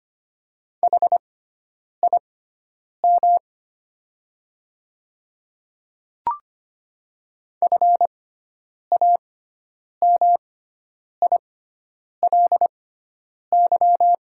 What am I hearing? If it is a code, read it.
Morse code, heard as a keyed tone, about 25 wpm, spaced as for 9 wpm, 700 Hz. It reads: HIM FAMILY
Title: 6x Character Spacing - Top 500 Words - 25wpm